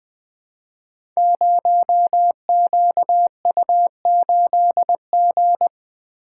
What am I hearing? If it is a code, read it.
Morse code, heard as a keyed tone, about 21 wpm, 700 Hz. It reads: 0QU8G
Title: Code Groups - 5 Letter:Numbers - 20wpm